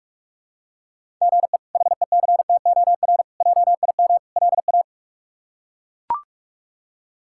Morse code, heard as a keyed tone, about 42 wpm, 700 Hz. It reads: GE HECTOR JIM LA
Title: K1USN SST QSO Practice - 45wpm